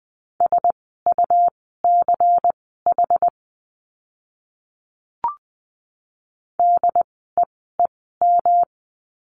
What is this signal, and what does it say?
Morse code, heard as a keyed tone, about 20 wpm, spaced as for 14 wpm, 700 Hz. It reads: SUCH DEEM
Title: 108 Lesson 18f - Letter m - Words 20wpm